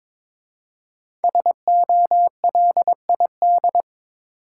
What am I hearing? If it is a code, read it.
Morse code, heard as a keyed tone, about 22 wpm, 700 Hz. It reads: SOLID
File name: 5 Letter Words - First 3 Letters Given - 22wpm